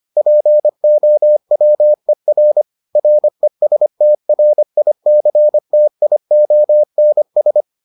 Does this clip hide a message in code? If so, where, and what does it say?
Yes: Morse code from 0.2 to 7.6 s, heard as a keyed tone, about 25 wpm, 600 Hz. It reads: POWER RESTRICTIONS